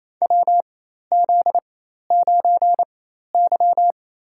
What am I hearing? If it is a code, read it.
Morse code, heard as a keyed tone, about 28 wpm, spaced as for 13 wpm, 700 Hz. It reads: WZ9Y
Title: USA Call Signs - 4 Characters - 4x Character Spacing - 28wpm